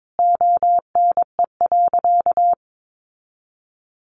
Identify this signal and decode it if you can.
Morse code, heard as a keyed tone, about 22 wpm, 700 Hz. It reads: ONE.
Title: Sentences from Top 700 Words 22wpm